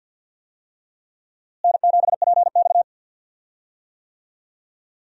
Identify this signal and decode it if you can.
Morse code, heard as a keyed tone, about 50 wpm, 700 Hz. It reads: N7PX
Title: USA Call Signs - 4 Characters - Repeated 2x - 50wpm